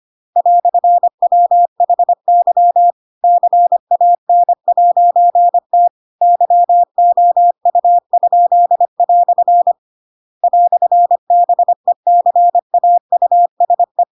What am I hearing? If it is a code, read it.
Morse code, heard as a keyed tone, about 25 wpm, 700 Hz. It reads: "WHY CAN'T YOU?" "BECAUSE
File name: Pinocchio / Pinocchio ch7 25 25 ik1hge